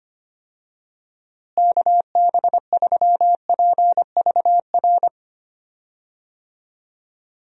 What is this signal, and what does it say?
Morse code, heard as a keyed tone, about 25 wpm, 700 Hz. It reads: KB3PVR
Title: USA Call Signs - 6 Characters - Repeated 2x - 25wpm